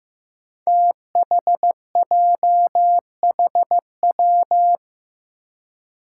Morse code, heard as a keyed tone, about 16 wpm, 700 Hz. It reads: THJHW
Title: Five Challenging Letter-Number Combinations 15wpm